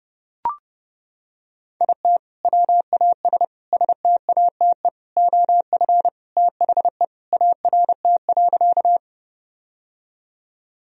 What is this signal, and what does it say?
Morse code, heard as a keyed tone, about 32 wpm, 700 Hz. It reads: IT WAS STATE OF THE ART.
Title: Sentences from Top 1000 Words - Review Part 3 - 30wpm